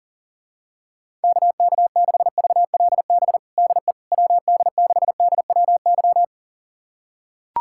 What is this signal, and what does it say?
Morse code, heard as a keyed tone, about 40 wpm, 700 Hz. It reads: KK6VLB DE WD6DWY
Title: USA Callsign DE Callsign - 6 Characters - 40wpm